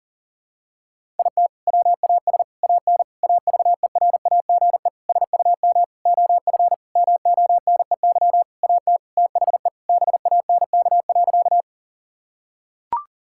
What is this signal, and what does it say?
Morse code, heard as a keyed tone, about 40 wpm, 700 Hz. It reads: IT WAS AN AVERAGE SUM OF MONEY AT THE BANK.
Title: Sentences from Top 1000 Words - Review Part 6 - 40wpm